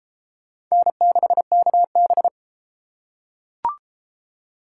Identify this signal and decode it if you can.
Morse code, heard as a keyed tone, about 29 wpm, 700 Hz. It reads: N6KB